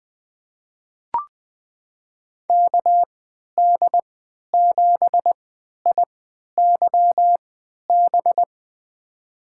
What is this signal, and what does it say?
Morse code, heard as a keyed tone, about 20 wpm, spaced as for 11 wpm, 700 Hz. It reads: KD7IYB
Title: USA Call Signs - 6 Characters - 3x Character Spacing - 20wpm